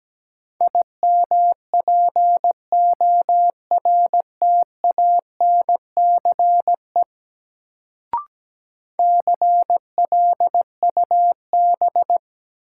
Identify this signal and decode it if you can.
Morse code, heard as a keyed tone, about 17 wpm, 700 Hz. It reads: IMPORTANCE CLUB
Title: Sentences from Top 1200 Words - New Word Drill - 17wpm